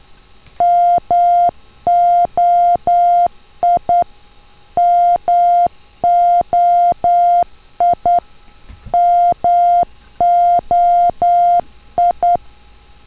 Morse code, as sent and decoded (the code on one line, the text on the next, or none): -- --- .. / -- --- .. / -- --- ..
MOI MOI MOI